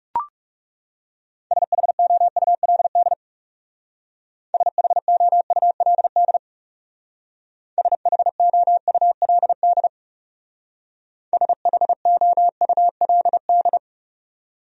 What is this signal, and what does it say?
Morse code, heard as a keyed tone, about 40 wpm, 700 Hz. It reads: SHOULD SHOULD SHOULD SHOULD